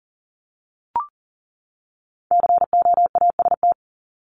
Morse code, heard as a keyed tone, about 40 wpm, 700 Hz. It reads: COAST
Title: Top 501-750 Words 40wpm